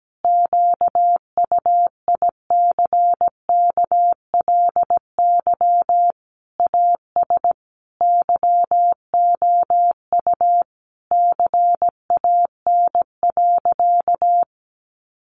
Morse code, heard as a keyed tone, about 17 wpm, 700 Hz. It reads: QUICKLY AS YOU CAN.
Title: Sentences from Top 1100 Words 17wpm